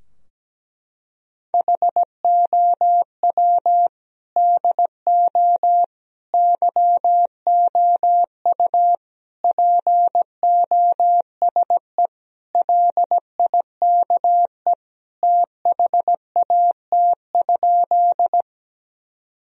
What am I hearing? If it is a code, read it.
Morse code, heard as a keyed tone, about 17 wpm, 700 Hz. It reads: HOW DO YOU POSE LIKE THAT?
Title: Sentences from Top 1000 Words - Review Part 2 - 17wpm